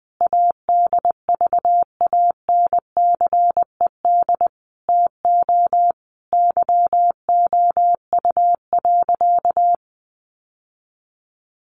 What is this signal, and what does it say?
Morse code, heard as a keyed tone, about 20 wpm, 700 Hz. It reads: ADVANCED TO YOU.
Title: Sentences from Top 1200 Words 20wpm